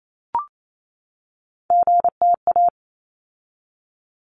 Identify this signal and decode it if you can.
Morse code, heard as a keyed tone, about 28 wpm, 700 Hz. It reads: GTA
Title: ARRL Sections 28wpm